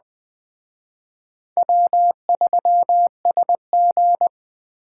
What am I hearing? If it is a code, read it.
Morse code, heard as a keyed tone, about 20 wpm, 700 Hz. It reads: W3SG